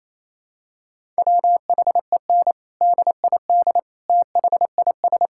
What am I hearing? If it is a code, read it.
Morse code, heard as a keyed tone, about 28 wpm, 700 Hz. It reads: WHEN DID THIS
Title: Sentences from Top 1000 Words - Review Part 6 - 28wpm